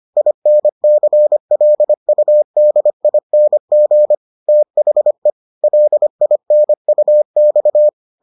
Morse code, heard as a keyed tone, about 25 wpm, 600 Hz. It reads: INCLUDING THE LINUX